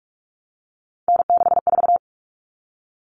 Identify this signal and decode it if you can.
Morse code, heard as a keyed tone, about 44 wpm, 700 Hz. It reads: N64
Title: Common Cut Numbers - 3-Digit 45wpm